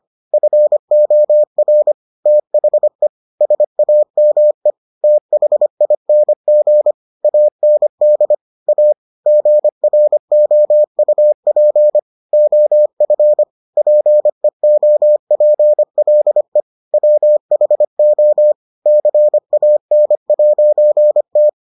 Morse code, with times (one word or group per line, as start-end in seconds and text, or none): 0.3-1.9 FOR
2.3-3.1 THE
3.4-4.7 SAME
5.0-6.9 THING
7.2-8.3 AND
8.7-8.9 A
9.3-12.0 GROUP
12.3-13.4 OF
13.8-16.6 PEOPLE
16.9-18.5 WHO
18.9-21.5 CAN'T